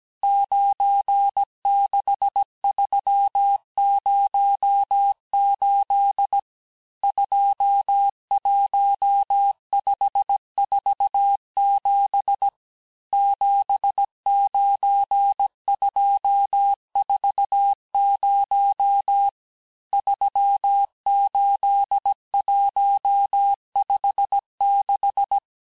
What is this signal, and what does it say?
Morse code, heard as a keyed tone, about 17 wpm, 800 Hz. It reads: 96308 21547 79240 38156
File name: cours-du-lundi files / cours-1587-texte-chiffre-vitesse-16MPM